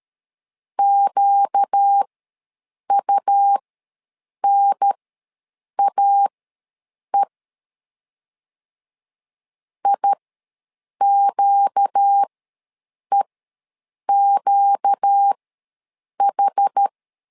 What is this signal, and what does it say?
Morse code, heard as a keyed tone, about 13 wpm, spaced as for 7 wpm, 800 Hz. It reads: QUNAE IQEQH
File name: LEZIONI / 4 Q ws 12wpm 1912